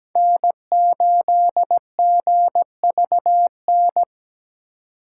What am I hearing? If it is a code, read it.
Morse code, heard as a keyed tone, about 17 wpm, 700 Hz. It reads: N8GVN
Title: USA Call Signs - 5 Characters - 8x Character Spacing - 17wpm